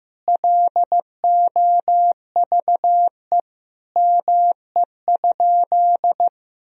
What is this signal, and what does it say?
Morse code, heard as a keyed tone, about 15 wpm, 700 Hz. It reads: LOVE ME?
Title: Sentences from Top 400 Words and Abbreviations 15wpm